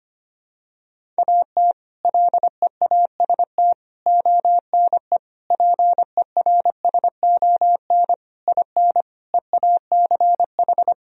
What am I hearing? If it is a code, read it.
Morse code, heard as a keyed tone, about 25 wpm, 700 Hz. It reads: AT LEAST ONE PERSON IN EACH
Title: Sentences from Top 1000 Words - Review Part 1 - 25wpm